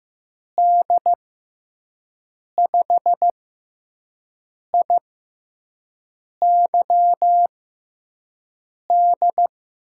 Morse code, heard as a keyed tone, about 15 wpm, spaced as for 5 wpm, 700 Hz. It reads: D5IYD
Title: USA Call Signs - 6 Characters - 6x Character Spacing - 15wpm